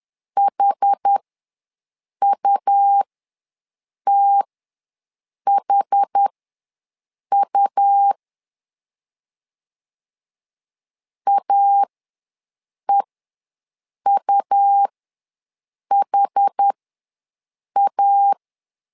Morse code, heard as a keyed tone, about 10 wpm, spaced as for 6 wpm, 800 Hz. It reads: HUTHU AEUHA
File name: LEZIONI / 2 UHDN / 2 UH ws 10wpm 2029